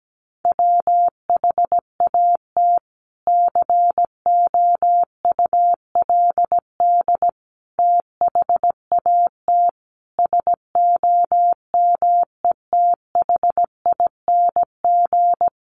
Morse code, heard as a keyed tone, about 17 wpm, 700 Hz. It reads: WHAT COULD THAT SOMETHING
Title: Sentences from Top 600 Words 17wpm